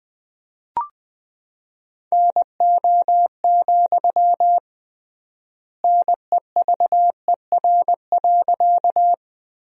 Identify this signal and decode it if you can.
Morse code, heard as a keyed tone, about 21 wpm, 700 Hz. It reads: NO, NEVER.
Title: 3x Word-Spacing - Sentences from Top 300 Words - 20wpm